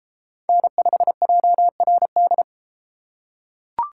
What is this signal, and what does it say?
Morse code, heard as a keyed tone, about 33 wpm, 700 Hz. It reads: N5JRD